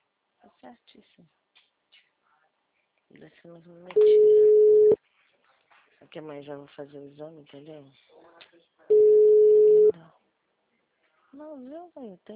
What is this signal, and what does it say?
Ringback tone (Europe, 425 Hz), with speech.
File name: out- -2035- - - .85483